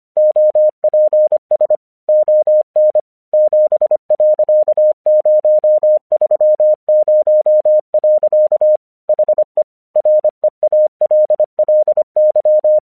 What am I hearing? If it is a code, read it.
Morse code, heard as a keyed tone, about 26 wpm, 610 Hz. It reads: OPS ON 7.030. HE REALLY
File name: 1 English 25wpm / AQC13